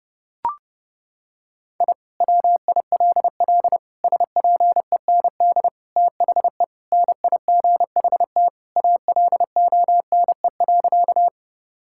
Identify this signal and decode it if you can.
Morse code, heard as a keyed tone, about 30 wpm, 700 Hz. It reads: I WILL SPEND THE NIGHT ALONE.
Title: Sentences from Top 1000 Words - Review Part 5 - 30wpm